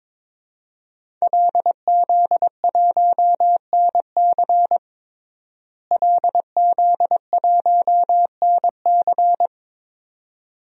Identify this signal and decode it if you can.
Morse code, heard as a keyed tone, about 22 wpm, 700 Hz. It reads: LZ1NC LZ1NC
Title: DXCC-100 Call Signs - 5 Characters - Repeated 2x - 22wpm